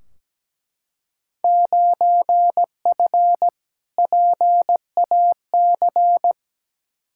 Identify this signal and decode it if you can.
Morse code, heard as a keyed tone, about 17 wpm, 700 Hz. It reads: 9F PAC
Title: Field Day Exchanges Encore - 17wpm